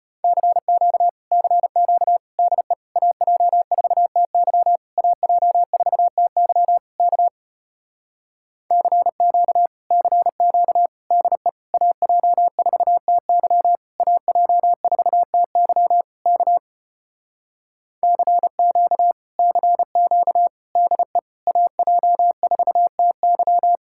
Morse code, heard as a keyed tone, about 33 wpm, 700 Hz. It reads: CQ CQ DE AJ4TY AJ4TY K CQ CQ DE AJ4TY AJ4TY K CQ CQ DE AJ4TY